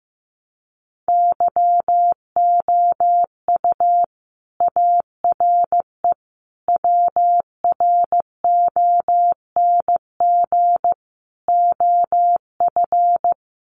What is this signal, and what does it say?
Morse code, heard as a keyed tone, about 15 wpm, 700 Hz. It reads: YOU ARE WRONG OF
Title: Sentences from Top 900 Words 15wpm